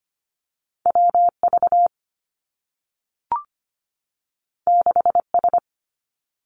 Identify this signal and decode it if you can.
Morse code, heard as a keyed tone, about 25 wpm, 700 Hz. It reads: WV 6S
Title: Two Challenging Letter-Number Combinations 25wpm